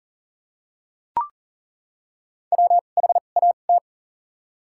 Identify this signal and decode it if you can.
Morse code, heard as a keyed tone, about 40 wpm, spaced as for 29 wpm, 700 Hz. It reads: WHAT
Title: 2x Character Spacing - Top 200 Words - 40wpm